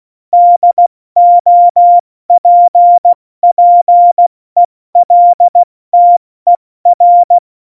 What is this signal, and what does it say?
Morse code, heard as a keyed tone, about 16 wpm, 700 Hz. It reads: DOPPELTER
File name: Morsekurs / Witze / Witz 03. November 2020, 70 F80